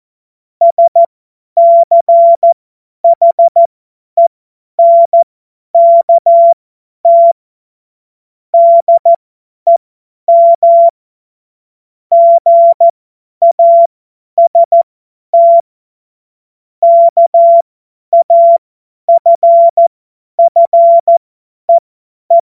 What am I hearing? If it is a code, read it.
Morse code, heard as a keyed tone, about 14 wpm, spaced as for 10 wpm, 680 Hz. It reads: SCHENKT DEM GAST KAFFEE
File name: Morsekurs / Witze / Witz 11. Mai 2021, 50 F70